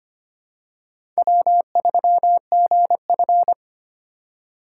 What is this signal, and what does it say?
Morse code, heard as a keyed tone, about 25 wpm, 700 Hz. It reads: W3GF
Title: USA Call Signs - 4x Character Spacing - 25wpm